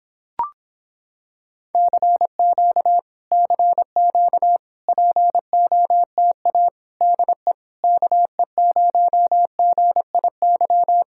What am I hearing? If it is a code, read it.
Morse code, heard as a keyed tone, about 26 wpm, 700 Hz. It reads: CQ CQ POTA DE KE0GIY